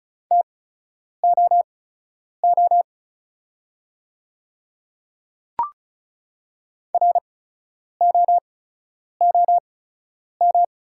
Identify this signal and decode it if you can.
Morse code, heard as a keyed tone, about 36 wpm, spaced as for 10 wpm, 700 Hz. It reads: TOO ROOM
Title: 8x Character Spacing - Top 300 Words - 35wpm